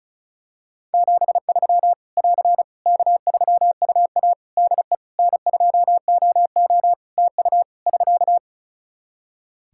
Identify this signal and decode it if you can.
Morse code, heard as a keyed tone, about 34 wpm, 700 Hz. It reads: 73 <AR> K3UA DE N2OO TU <SK>
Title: K1USN SST QSO Practice 35wpm